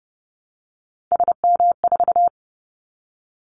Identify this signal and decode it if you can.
Morse code, heard as a keyed tone, about 30 wpm, 700 Hz. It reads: SM4
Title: Callsign Prefixes for Top 25 DXCC - Repeated 3x - 30wpm